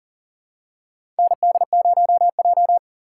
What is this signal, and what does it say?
Morse code, heard as a keyed tone, about 40 wpm, 700 Hz. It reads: ND0J
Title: USA Call Signs - 4 Characters - 2x Character Spacing - 40wpm